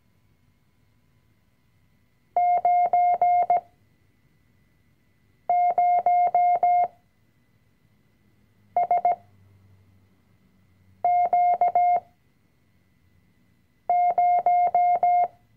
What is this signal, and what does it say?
Morse code, heard as a keyed tone, about 17 wpm, spaced as for 4 wpm, 710 Hz. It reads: 90SQ0
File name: CW-Online-Kurs / Anfänger / Lektion 005